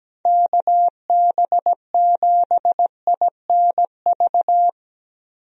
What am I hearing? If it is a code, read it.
Morse code, heard as a keyed tone, about 17 wpm, 700 Hz. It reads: KB7INV
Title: USA Call Signs - 6 Characters - Repeated 2x - 17wpm